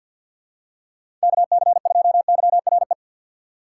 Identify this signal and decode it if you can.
Morse code, heard as a keyed tone, about 51 wpm, 690 Hz. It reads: KC2YRE
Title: USA Call Signs - 6 Characters - 3x Character Spacing - 50wpm